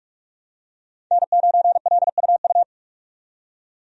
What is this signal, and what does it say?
Morse code, heard as a keyed tone, about 43 wpm, 700 Hz. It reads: N9LUU
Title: USA Call Signs - 5 Characters - 45wpm